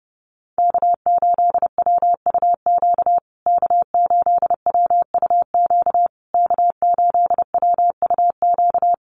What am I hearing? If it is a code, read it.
Morse code, heard as a keyed tone, about 30 wpm, 700 Hz. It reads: K8WUQ K8WUQ K8WUQ
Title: USA Call Signs Repeated 3x 30wpm